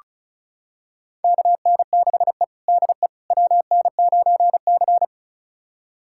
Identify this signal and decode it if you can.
Morse code, heard as a keyed tone, about 36 wpm, 700 Hz. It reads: KN6E DE WN9C
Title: USA Callsign DE Callsign - 4 Characters - 35wpm